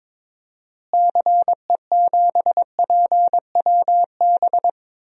Morse code, heard as a keyed tone, about 22 wpm, 700 Hz. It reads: CE7PWB
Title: DXCC-100 Call Signs - 6 Characters - Repeated 2x - 22wpm